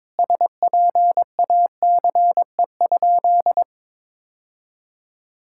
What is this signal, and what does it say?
Morse code, heard as a keyed tone, about 22 wpm, 700 Hz. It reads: SPACE?
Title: Sentences from Top 1000 Words - Review Part 5 - 22wpm